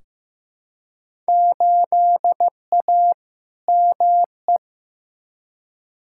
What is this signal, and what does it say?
Morse code, heard as a keyed tone, about 15 wpm, 700 Hz. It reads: 8A ME
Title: Field Day Exchanges - 15wpm